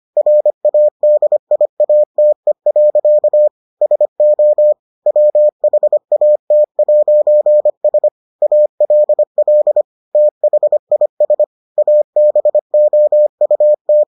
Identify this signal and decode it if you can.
Morse code, heard as a keyed tone, about 25 wpm, 600 Hz. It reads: RADIATE. SO WHAT'S ALL THIS ABOUT